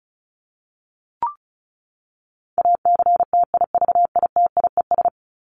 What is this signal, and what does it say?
Morse code, heard as a keyed tone, about 34 wpm, 700 Hz. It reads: ACTIVITIES
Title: ES:S Suffix - Top 100 Words with Base Words 35wpm